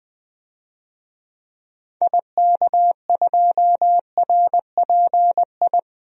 Morse code, heard as a keyed tone, about 20 wpm, 700 Hz. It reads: IK2RPI
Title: DXCC-100 Call Signs - 6 Characters - Repeated 2x - 20wpm